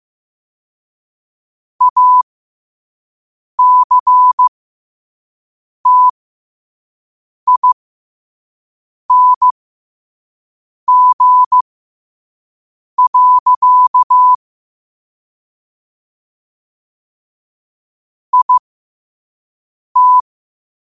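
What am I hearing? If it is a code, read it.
Morse code, heard as a keyed tone, about 15 wpm, spaced as for 5 wpm, 1000 Hz. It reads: ACTING. IT
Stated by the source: Morse code quote of the day at 5 wpm.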